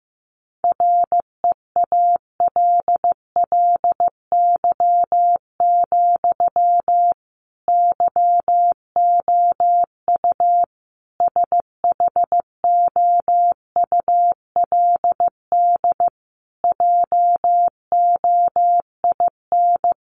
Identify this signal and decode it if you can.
Morse code, heard as a keyed tone, about 15 wpm, 700 Hz. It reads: REALLY, YOU SHOULD JOIN
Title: Sentences from Top 1400 Words 15wpm